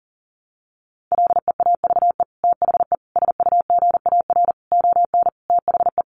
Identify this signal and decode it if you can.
Morse code, heard as a keyed tone, about 40 wpm, 700 Hz. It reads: LEAVE THE SUGAR ON THE
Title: Sentences from Top 1200 Words 40wpm